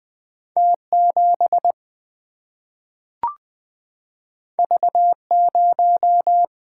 Morse code, heard as a keyed tone, about 20 wpm, 700 Hz. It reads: T 7 V 0
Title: Single-Letter Single-Number 20wpm